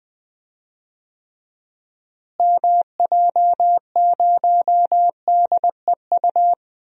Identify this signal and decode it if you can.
Morse code, heard as a keyed tone, about 21 wpm, 700 Hz. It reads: MJ0DEU
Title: DXCC-100 Call Signs - 6 Characters - Repeated 2x - 20wpm